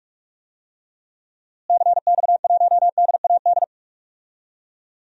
Morse code, heard as a keyed tone, about 44 wpm, 700 Hz. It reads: CX1DAD
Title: DXCC-100 Call Signs - 6 Characters - Repeated 2x - 45wpm